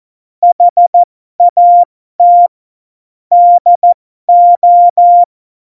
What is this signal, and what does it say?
Morse code, heard as a keyed tone, about 14 wpm, spaced as for 12 wpm, 700 Hz. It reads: HAT DO
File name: Morsekurs / Witze